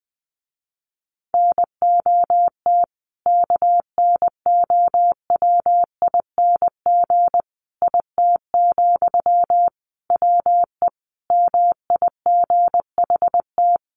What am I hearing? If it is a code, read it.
Morse code, heard as a keyed tone, about 20 wpm, 700 Hz. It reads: NOT KNOWING IT, WE MIGHT
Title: A Sound Of Thunder 20wpm